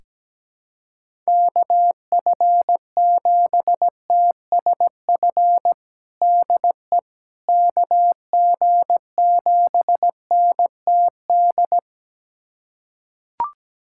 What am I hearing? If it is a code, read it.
Morse code, heard as a keyed tone, about 17 wpm, 700 Hz. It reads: KF7TSF DE KG7NTD
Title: USA Callsign DE Callsign - 6 Characters - 17wpm